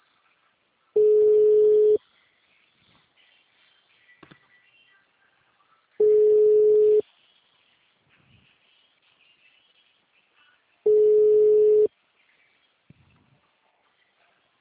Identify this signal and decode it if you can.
Ringback tone (Europe, 425 Hz).